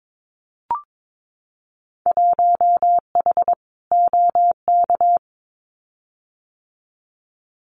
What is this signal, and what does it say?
Morse code, heard as a keyed tone, about 22 wpm, 700 Hz. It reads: 1H OK
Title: Winter Field Day Exchanges 22wpm